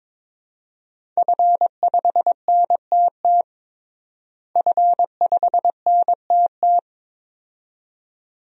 Morse code, heard as a keyed tone, about 22 wpm, 700 Hz. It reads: F5NTT F5NTT
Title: DXCC-100 Call Signs - 5 Characters - Repeated 2x - 22wpm